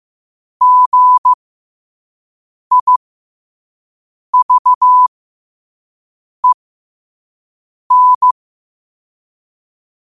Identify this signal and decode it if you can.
Morse code, heard as a keyed tone, about 15 wpm, spaced as for 5 wpm, 1000 Hz. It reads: GIVEN
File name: qotdfiles / 5wpm